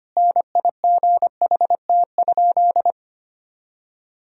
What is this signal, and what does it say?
Morse code, heard as a keyed tone, about 25 wpm, 700 Hz. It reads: NIGHT?
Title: Sentences from Top 1000 Words - Review Part 1 - 25wpm